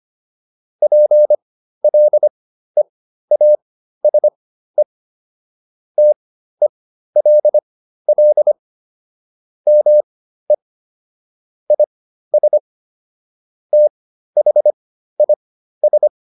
Morse code, heard as a keyed tone, about 25 wpm, spaced as for 13 wpm, 600 Hz. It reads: PLEASE TELL ME IS THIS